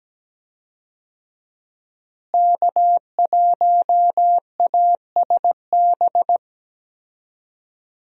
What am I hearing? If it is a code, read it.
Morse code, heard as a keyed tone, about 17 wpm, 700 Hz. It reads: K1ASB
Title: USA Call Signs - 5 Characters - Repeated 3x - 17wpm